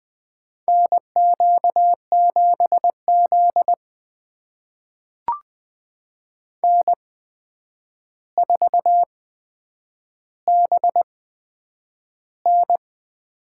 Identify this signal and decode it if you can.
Morse code, heard as a keyed tone, about 20 wpm, spaced as for 9 wpm, 700 Hz. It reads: NQ7Z N4BN
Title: USA Call Signs - 4 Characters - 8x Character Spacing - 20wpm